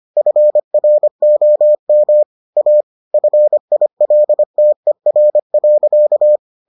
Morse code, heard as a keyed tone, about 25 wpm, 600 Hz. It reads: FROM A FILTER.